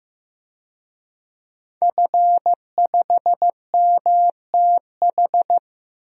Morse code, heard as a keyed tone, about 15 wpm, 700 Hz. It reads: F5MTH
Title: DXCC-100 Call Signs - 5 Characters - Repeated 2x - 15wpm